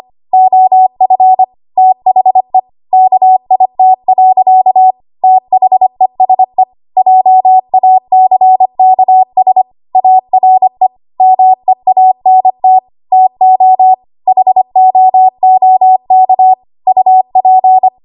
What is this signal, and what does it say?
Morse code, heard as a keyed tone, about 25 wpm, 750 Hz. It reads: OF THE KIT. THESE JACKS ARE MEANT TO HOOK UP
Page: 25 WPM Code Archive